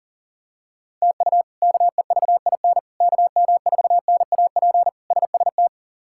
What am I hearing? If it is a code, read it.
Morse code, heard as a keyed tone, about 40 wpm, 700 Hz. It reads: TU KEVIN KM4NAP SST
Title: K1USN SST QSO Practice - 40wpm